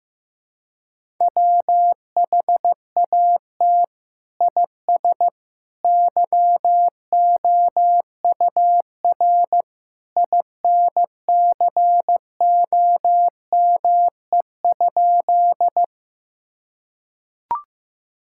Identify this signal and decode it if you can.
Morse code, heard as a keyed tone, about 16 wpm, 700 Hz. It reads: WHAT IS YOUR INCOME?
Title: Sentences from Top 1000 Words - Review Part 5 - 15wpm